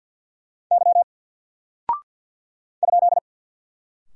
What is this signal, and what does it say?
Morse code, heard as a keyed tone, about 48 wpm, 700 Hz. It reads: Y ?
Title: Single Character (Letter:Number:Punctuation:Prosign) Alt 50wpm